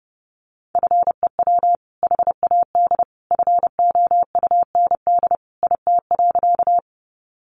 Morse code, heard as a keyed tone, about 30 wpm, 700 Hz. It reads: FEW HAD FOUND IT.
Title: Sentences from Top 300 Words 30wpm